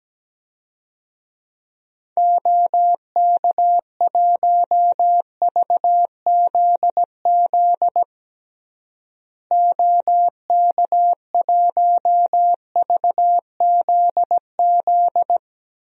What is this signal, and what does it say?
Morse code, heard as a keyed tone, about 17 wpm, 700 Hz. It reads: OK1VZZ OK1VZZ
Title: DXCC-100 Call Signs - 6 Characters - Repeated 2x - 17wpm